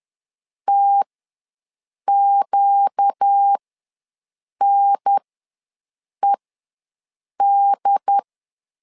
Morse code, heard as a keyed tone, about 10 wpm, spaced as for 6 wpm, 800 Hz. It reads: TQNED
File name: LEZIONI / 4 Q ws 10wpm 2306